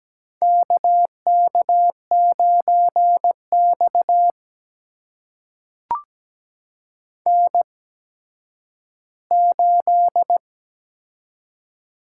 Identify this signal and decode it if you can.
Morse code, heard as a keyed tone, about 17 wpm, spaced as for 10 wpm, 700 Hz. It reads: KK9X N8
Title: USA Call Signs - 4 Characters - 8x Character Spacing - 17wpm